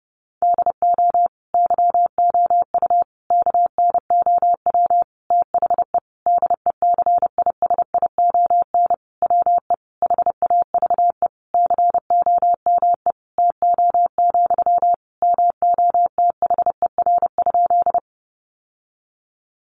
Morse code, heard as a keyed tone, about 30 wpm, 700 Hz. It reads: DO YOU KNOW THE DECISION WE HAVE COME TO, MOTHER?
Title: Sentences from Top 1000 Words 30wpm